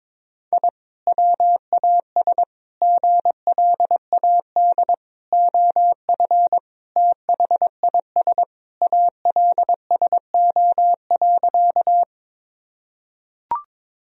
Morse code, heard as a keyed tone, about 22 wpm, 700 Hz. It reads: I WAS GLAD OF THIS ALSO.
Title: Sentences from Top 1000 Words - Review Part 1 - 22wpm